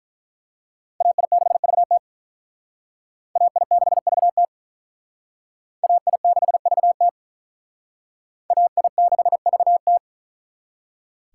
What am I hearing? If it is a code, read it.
Morse code, heard as a keyed tone, about 43 wpm, 700 Hz. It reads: AI6VT AI6VT AI6VT AI6VT
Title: Speed Racing - USA 2x2 Call Signs 35wpm